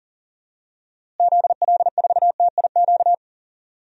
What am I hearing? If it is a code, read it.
Morse code, heard as a keyed tone, about 37 wpm, 700 Hz. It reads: ZL4TIQ
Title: DXCC-100 Call Signs - 6 Characters - Repeated 3x - 40wpm